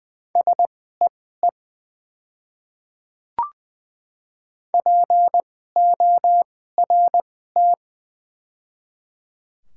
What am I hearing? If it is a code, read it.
Morse code, heard as a keyed tone, about 20 wpm, spaced as for 14 wpm, 700 Hz. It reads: SEE PORT
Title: 2x Character Spacing - Top 300 Words - 20wpm